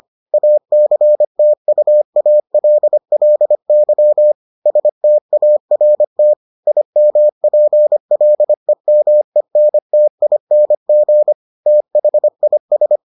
Morse code, heard as a keyed tone, about 25 wpm, 600 Hz. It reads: ACTUALLY START IMPLEMENTING THIS